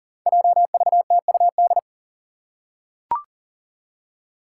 Morse code, heard as a keyed tone, about 39 wpm, 700 Hz. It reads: JVTUD
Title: Five Challenging Letter-Number Combinations 40wpm